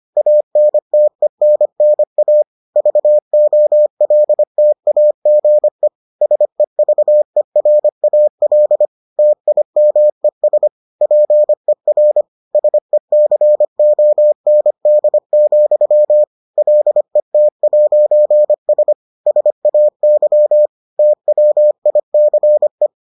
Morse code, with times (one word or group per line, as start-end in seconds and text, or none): 0.2-2.4 ANTENNA
2.8-5.9 VOLTAGE
6.2-8.9 SEVERAL
9.2-10.7 TIMES
11.0-12.2 PER
12.5-16.2 SECOND,
16.6-18.9 LET'S
19.3-20.7 SAY
21.0-22.9 TWICE